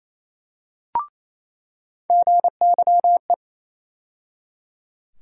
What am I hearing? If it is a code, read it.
Morse code, heard as a keyed tone, about 28 wpm, 700 Hz. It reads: GYE
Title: USA Call Sign Postfixes - 3 Letters 28wpm